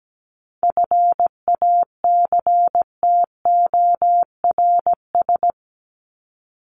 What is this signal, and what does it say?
Morse code, heard as a keyed tone, about 17 wpm, 700 Hz. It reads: FACTORS
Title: ES:S Suffix - Top 100 Words with Base Words 17wpm